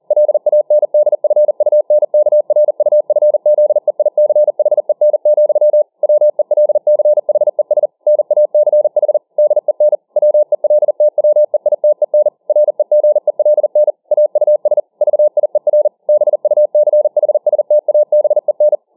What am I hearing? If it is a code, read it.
Morse code, heard as a keyed tone, about 40 wpm, 600 Hz. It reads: LANDFUNKRUFZEICHEN, WELCHES NACH DEN WELTWEITEN REGELN AUS VIER BUCHSTABEN